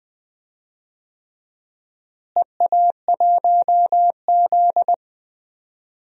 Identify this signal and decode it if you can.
Morse code, heard as a keyed tone, about 21 wpm, 700 Hz. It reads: EA1Z
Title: DXCC-100 Call Signs - 4 Characters - Repeated 3x - 20wpm